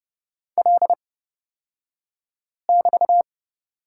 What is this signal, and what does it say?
Morse code, heard as a keyed tone, about 29 wpm, spaced as for 5 wpm, 690 Hz. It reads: L<BT>
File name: Lesson 36d - BT - Two Characters 30wpm